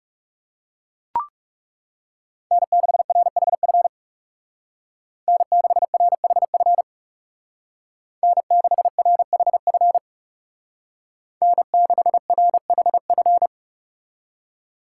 Morse code, heard as a keyed tone, about 35 wpm, 700 Hz. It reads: N6RHF N6RHF N6RHF N6RHF
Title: Speed Racing - USA 1x3 Call Signs 30wpm